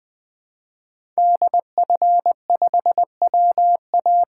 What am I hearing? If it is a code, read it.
Morse code, heard as a keyed tone, about 20 wpm, 700 Hz. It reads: DF5WA
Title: 2023 SOTA Chaser Call Signs - 20wpm